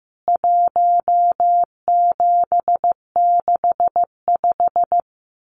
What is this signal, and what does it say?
Morse code, heard as a keyed tone, about 14 wpm, 700 Hz. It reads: 1765
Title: CWT Exchange 15wpm